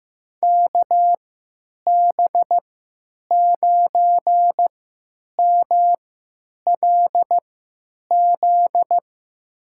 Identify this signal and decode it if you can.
Morse code, heard as a keyed tone, about 15 wpm, spaced as for 9 wpm, 700 Hz. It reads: KB9MLZ
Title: USA Call Signs - 6 Characters - 3x Character Spacing - 15wpm